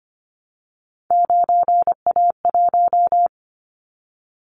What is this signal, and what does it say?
Morse code, heard as a keyed tone, about 25 wpm, 700 Hz. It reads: 9A1
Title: Callsign Prefixes for Top 25 DXCC - 25wpm